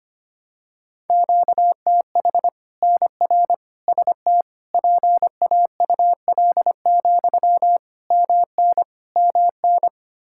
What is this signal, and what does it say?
Morse code, heard as a keyed tone, about 25 wpm, 700 Hz. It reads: QTH NR ST PAUL, MN MN
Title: Practice QSO - Exchange 1 - Beginning - 25wpm